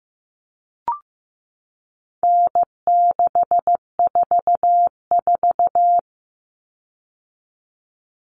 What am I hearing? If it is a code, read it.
Morse code, heard as a keyed tone, about 15 wpm, 700 Hz. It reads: N644
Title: Common Cut Numbers - 4-Digit 15wpm